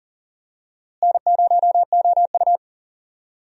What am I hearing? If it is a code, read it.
Morse code, heard as a keyed tone, about 40 wpm, 700 Hz. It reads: N0OU
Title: USA Call Signs - 4 Characters - Repeated 3x - 40wpm